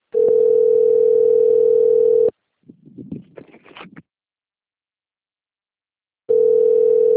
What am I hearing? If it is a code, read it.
Ringback tone (North America), 440 + 480 Hz.
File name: rg-1600-unknown- - - .5657